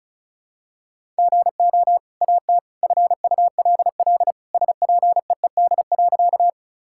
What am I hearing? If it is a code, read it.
Morse code, heard as a keyed tone, about 34 wpm, 700 Hz. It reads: GO AT FULL SPEED.
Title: Sentences from Top 1000 Words - Review Part 5 - 35wpm